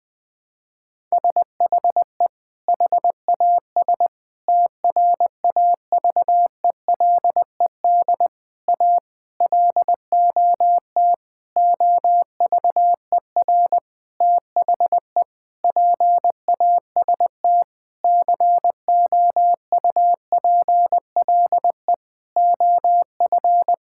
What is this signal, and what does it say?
Morse code, heard as a keyed tone, about 20 wpm, 700 Hz. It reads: SHE HAS TRAVELED A LOT OVER THE PAST COUPLE OF
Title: Sentences with 6-Word Phrases - 20wpm